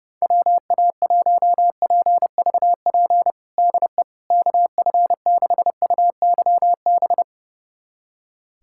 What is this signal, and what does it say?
Morse code, heard as a keyed tone, about 30 wpm, 700 Hz. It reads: WA1PVP DE KF6UYB
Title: USA Callsign DE Callsign - 6 Characters - 30wpm